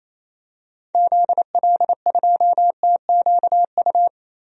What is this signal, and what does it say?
Morse code, heard as a keyed tone, about 28 wpm, 700 Hz. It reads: ZL2TQU
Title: DXCC-100 Call Signs - 6 Characters - Repeated 2x - 28wpm